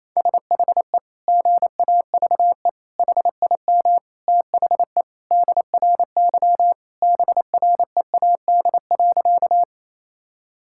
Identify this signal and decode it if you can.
Morse code, heard as a keyed tone, about 28 wpm, 700 Hz. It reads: SHE GAVE HIM THE DRY BREAD.
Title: Sentences from Top 1600 Words 28wpm